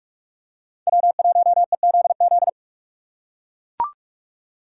Morse code, heard as a keyed tone, about 44 wpm, 700 Hz. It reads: W1EZZ